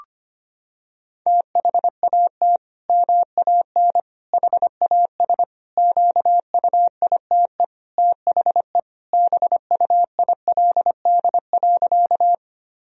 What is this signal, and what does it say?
Morse code, heard as a keyed tone, about 25 wpm, 700 Hz. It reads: THAT MAN HAS QUITE THE BUILD.
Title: Sentences from Top 1000 Words - Review Part 1 - 25wpm